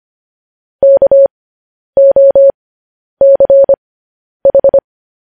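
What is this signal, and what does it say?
Morse code, heard as a keyed tone, about 25 wpm, spaced as for 10 wpm, 550 Hz. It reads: KOCH
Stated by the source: Morse code news headlines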